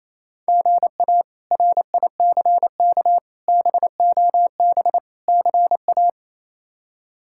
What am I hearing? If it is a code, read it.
Morse code, heard as a keyed tone, about 28 wpm, 700 Hz. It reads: GA RICK BOB CA
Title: K1USN SST QSO Practice - 28wpm